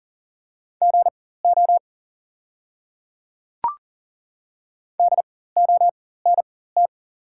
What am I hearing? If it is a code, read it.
Morse code, heard as a keyed tone, about 40 wpm, spaced as for 19 wpm, 690 Hz. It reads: GO DONT
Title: 4x Character Spacing - Top 500 Words - 40wpm